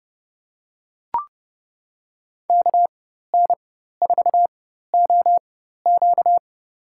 Morse code, heard as a keyed tone, about 30 wpm, spaced as for 14 wpm, 700 Hz. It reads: KN4OQ
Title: USA Call Signs - 4x Character Spacing - 30wpm